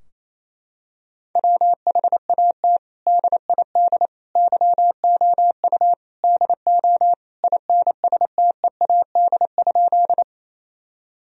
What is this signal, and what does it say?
Morse code, heard as a keyed tone, about 27 wpm, 700 Hz. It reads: WHAT DID YOU DO INSTEAD?
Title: Sentences from Top 1000 Words - Review Part 4 - 28wpm